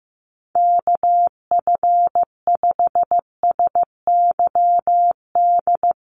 Morse code, heard as a keyed tone, about 15 wpm, 700 Hz. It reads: KF5SYD
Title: USA Call Signs Repeated 2x 15wpm